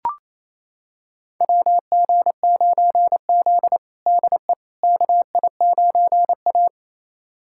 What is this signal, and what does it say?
Morse code, heard as a keyed tone, about 28 wpm, 700 Hz. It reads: WG9Z DE KI9A
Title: USA Callsign DE Callsign - 4 Characters - 28wpm